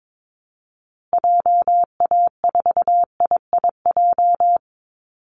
Morse code, heard as a keyed tone, about 22 wpm, 700 Hz. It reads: JA4IIJ
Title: Call Signs Repeated 2x - 22wpm